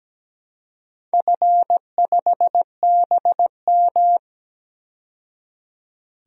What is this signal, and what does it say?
Morse code, heard as a keyed tone, about 17 wpm, 700 Hz. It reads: F5BM
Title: DXCC-100 Call Signs - 4 Characters - 17wpm